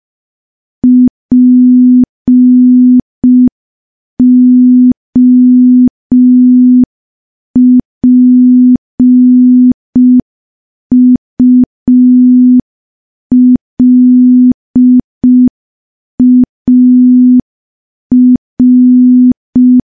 Morse code, heard as a keyed tone, about 5 wpm, 250 Hz. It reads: POPULAR
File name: morse / morse-training-05WPM